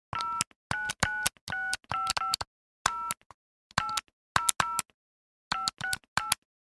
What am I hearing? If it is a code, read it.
DTMF, heard as telephone touch tones, tones of about 160 ms. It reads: 0996550 800 568
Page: Dialling buttons on touch tone cell phone.